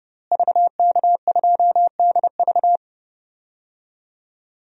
Morse code, heard as a keyed tone, about 30 wpm, 700 Hz. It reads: VK2DV
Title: DXCC-100 Call Signs - 5 Characters - Repeated 2x - 30wpm